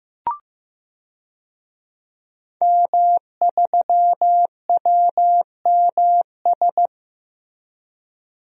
Morse code, heard as a keyed tone, about 15 wpm, 700 Hz. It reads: M3WMS
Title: DXCC-100 Call Signs - 5 Characters - Repeated 3x - 15wpm